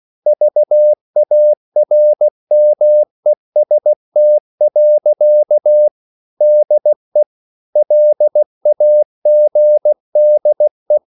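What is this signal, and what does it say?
Morse code, heard as a keyed tone, about 16 wpm, 600 Hz. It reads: VARMEST. DE LAGDE
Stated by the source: Slow (QRS) Morse code at 16 wpm, from CW med Gnister.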